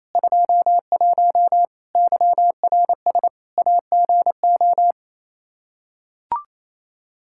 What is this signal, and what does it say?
Morse code, heard as a keyed tone, about 28 wpm, 700 Hz. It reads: 21 YRS AGO
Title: Practice QSO - Exchange 3 - More Info - 28wpm